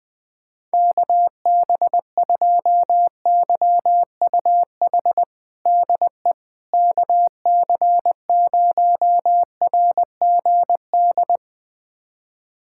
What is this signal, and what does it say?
Morse code, heard as a keyed tone, about 20 wpm, 700 Hz. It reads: KB2YUH DE KC0RGD
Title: USA Callsign DE Callsign - 6 Characters - 20wpm